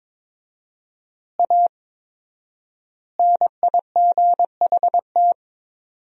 Morse code, heard as a keyed tone, about 22 wpm, 700 Hz. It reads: A NIGHT
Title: 4x Word-Spacing - Sentences from Top 300 Words - 22wpm